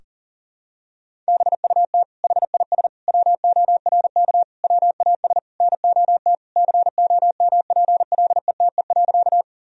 Morse code, heard as a keyed tone, about 37 wpm, 700 Hz. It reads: BUT HIS WORK WAS NOT COMPLETE.
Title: Sentences from Top 1000 Words - Review Part 2 - 40wpm